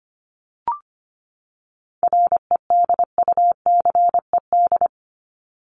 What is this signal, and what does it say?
Morse code, heard as a keyed tone, about 25 wpm, 700 Hz. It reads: REDUCED
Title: ED Suffix - Top 100 Words with Base Words 25wpm